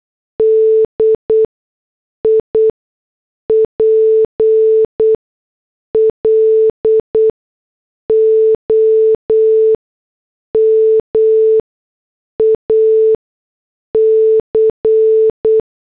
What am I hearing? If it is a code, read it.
Morse code, heard as a keyed tone, about 8 wpm, spaced as for 6 wpm, 440 Hz. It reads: DIPLOMAC